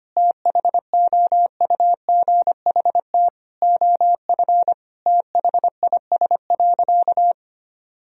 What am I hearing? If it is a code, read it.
Morse code, heard as a keyed tone, about 25 wpm, 700 Hz. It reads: THOUGHT OF THIS.
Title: Sentences from Top 1000 Words - Review Part 1 - 25wpm